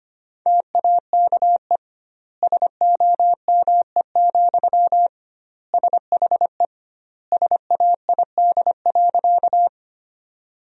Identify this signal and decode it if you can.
Morse code, heard as a keyed tone, about 25 wpm, 700 Hz. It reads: TAKE SOME, SHE SAID.
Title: 2x Word-Spacing - Sentences from Top 200 Words - 25wpm